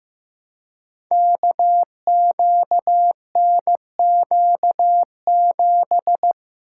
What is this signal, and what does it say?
Morse code, heard as a keyed tone, about 16 wpm, 700 Hz. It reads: KQNQ7
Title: Code Groups - 5 Letter:Numbers - 15wpm